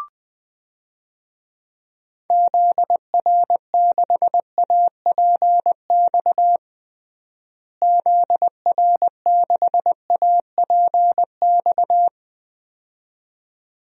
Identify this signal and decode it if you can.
Morse code, heard as a keyed tone, about 20 wpm, 700 Hz. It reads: ZR6APX ZR6APX
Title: DXCC-100 Call Signs - 6 Characters - Repeated 2x - 20wpm